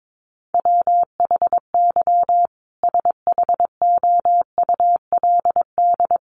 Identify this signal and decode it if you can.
Morse code, heard as a keyed tone, about 22 wpm, 700 Hz. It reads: WHY SHOULD
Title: Sentences from Top 1500 Words 22wpm